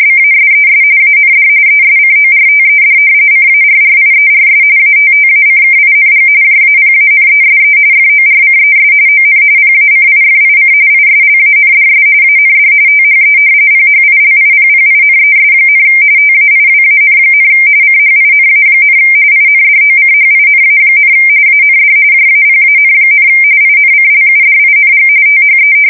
RTTY, about 45.45 baud. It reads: BASES AND ESCORTED CARRIERS ATTACKING JAPANESE FORCES ON PACIFIC ISLANDS AND THE PHILIPPINES, TAIWAN, CHINA, INDOCHINA, AND JAPAN, INCLUDING THE
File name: fsk170 / battleshipnj